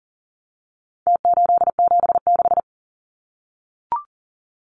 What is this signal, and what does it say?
Morse code, heard as a keyed tone, about 40 wpm, 700 Hz. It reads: T876
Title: Numbers - Contest Style - 4-Digit 40wpm